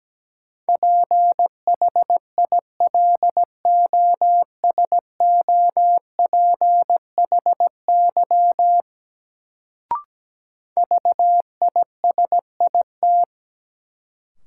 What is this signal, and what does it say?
Morse code, heard as a keyed tone, about 17 wpm, 700 Hz. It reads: PHILOSOPHY VISIT
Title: Sentences from Top 900 Words - New Word Drill - 17wpm